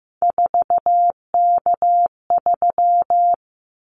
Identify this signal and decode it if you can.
Morse code, heard as a keyed tone, about 15 wpm, 700 Hz. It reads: 4K3
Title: Callsign Prefixes for Top 100 DXCC - 15wpm